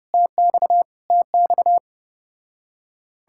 Morse code, heard as a keyed tone, about 33 wpm, 700 Hz. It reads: TX TX